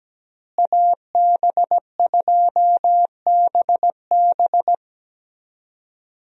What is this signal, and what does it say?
Morse code, heard as a keyed tone, about 17 wpm, 700 Hz. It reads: AB2BB
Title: USA Call Signs - 5 Characters - Repeated 3x - 17wpm